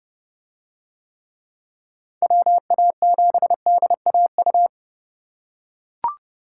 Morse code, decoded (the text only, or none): WA7DAU